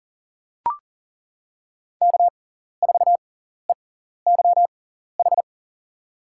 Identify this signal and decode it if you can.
Morse code, heard as a keyed tone, about 40 wpm, spaced as for 14 wpm, 700 Hz. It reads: K4EYH
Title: USA Call Signs - 5 Characters - 6x Character Spacing - 40wpm